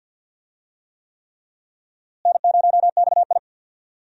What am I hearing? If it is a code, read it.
Morse code, heard as a keyed tone, about 49 wpm, 700 Hz. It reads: N0XI
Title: USA Call Signs - 4 Characters - Repeated 3x - 50wpm